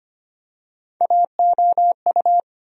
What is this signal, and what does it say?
Morse code, heard as a keyed tone, about 25 wpm, 700 Hz. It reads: AOU
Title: USA Call Sign Postfixes - 3 Letters 25wpm